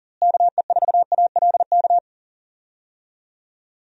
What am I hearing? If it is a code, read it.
Morse code, heard as a keyed tone, about 40 wpm, 700 Hz. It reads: KE4ALK
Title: USA Call Signs - 6 Characters - Repeated 3x - 40wpm